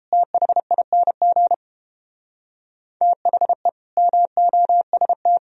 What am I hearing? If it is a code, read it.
Morse code, heard as a keyed tone, about 32 wpm, 700 Hz. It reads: THING THE MOST